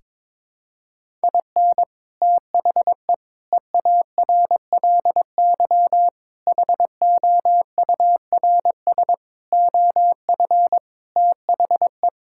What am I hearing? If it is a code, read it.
Morse code, heard as a keyed tone, about 22 wpm, 700 Hz. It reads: IN THE EARLY HOURS OF THE
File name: practice / 22wpm / Sets of 7 Words - 22wpm